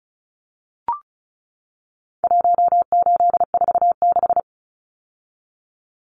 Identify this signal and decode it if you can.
Morse code, heard as a keyed tone, about 34 wpm, 700 Hz. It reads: 1846
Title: Numbers - Contest Style - 4-Digit 35wpm